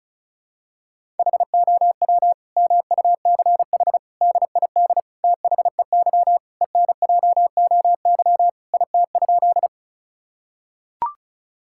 Morse code, heard as a keyed tone, about 34 wpm, 700 Hz. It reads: HOW MUCH DID THEY ENJOY IT?
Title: Sentences from Top 1000 Words - Review Part 5 - 35wpm